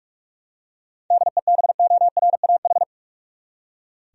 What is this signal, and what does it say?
Morse code, heard as a keyed tone, about 46 wpm, 700 Hz. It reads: DEBORAH
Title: USA Common Names - 45wpm